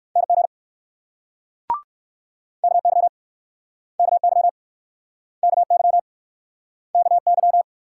Morse code, heard as a keyed tone, about 61 wpm, 690 Hz. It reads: NG KY KY KY KY